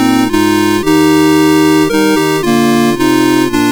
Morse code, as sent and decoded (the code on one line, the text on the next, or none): none